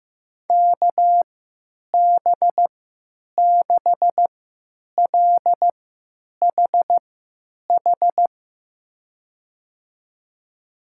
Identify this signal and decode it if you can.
Morse code, heard as a keyed tone, about 15 wpm, spaced as for 9 wpm, 700 Hz. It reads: KB6LHH
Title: USA Call Signs - 3x Character Spacing - 15wpm